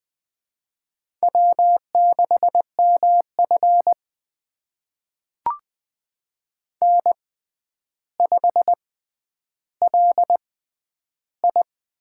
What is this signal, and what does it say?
Morse code, heard as a keyed tone, about 21 wpm, spaced as for 10 wpm, 700 Hz. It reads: W6MF N5LI
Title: USA Call Signs - 4 Characters - 6x Character Spacing - 20wpm